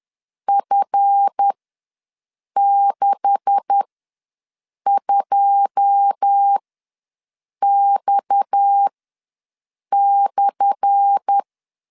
Morse code, heard as a keyed tone, about 10 wpm, spaced as for 6 wpm, 800 Hz. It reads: F62X/
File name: LEZIONI / 10 SIGNS / 10A SLASH ws 10wpm 4121